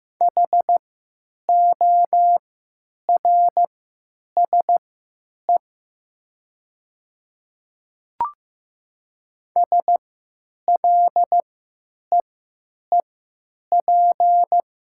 Morse code, heard as a keyed tone, about 15 wpm, spaced as for 9 wpm, 700 Hz. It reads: HORSE SLEEP
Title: 3x Character Spacing - Top 500 Words - 15wpm